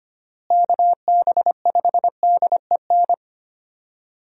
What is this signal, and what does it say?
Morse code, heard as a keyed tone, about 25 wpm, 700 Hz. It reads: KB5DEN
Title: USA Call Signs - 6 Characters - Repeated 2x - 25wpm